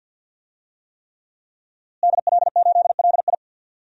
Morse code, heard as a keyed tone, about 47 wpm, 700 Hz. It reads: DL8LI
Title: DXCC-100 Call Signs - 5 Characters - Repeated 3x - 50wpm